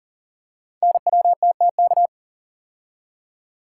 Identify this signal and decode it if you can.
Morse code, heard as a keyed tone, about 38 wpm, 690 Hz. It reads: NWTTK
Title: Code Groups - 5 Letters - 40wpm